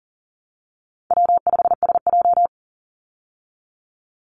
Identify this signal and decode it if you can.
Morse code, heard as a keyed tone, about 40 wpm, 700 Hz. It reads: W5SJ
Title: Easy Call Signs Repeated 2x - 40wpm